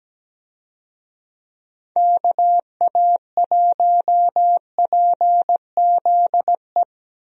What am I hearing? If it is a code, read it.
Morse code, heard as a keyed tone, about 17 wpm, 700 Hz. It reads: KA1PZE
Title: USA Call Signs - 6 Characters - Repeated 3x - 17wpm